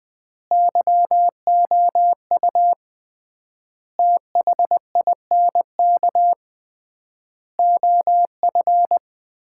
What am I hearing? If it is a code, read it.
Morse code, heard as a keyed tone, about 20 wpm, 700 Hz. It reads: YOU THINK OF
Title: 3x Word-Spacing - Sets of 5 Words - 20wpm